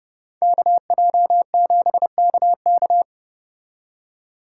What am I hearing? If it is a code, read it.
Morse code, heard as a keyed tone, about 34 wpm, 700 Hz. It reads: KJ7KK